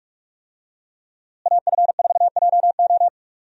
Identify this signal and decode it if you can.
Morse code, heard as a keyed tone, about 46 wpm, 700 Hz. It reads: AF4JO
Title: USA Call Signs - 5 Characters - Repeated 3x - 45wpm